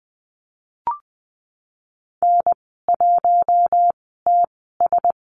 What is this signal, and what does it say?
Morse code, heard as a keyed tone, about 20 wpm, spaced as for 15 wpm, 700 Hz. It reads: N1TS
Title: 048 Lesson 9g - Number 4 - Call Signs 20wpm